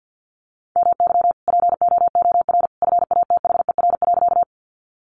Morse code, heard as a keyed tone, about 50 wpm, 700 Hz. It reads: MY POOR FATHER.
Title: Sentences from Top 1000 Words 50wpm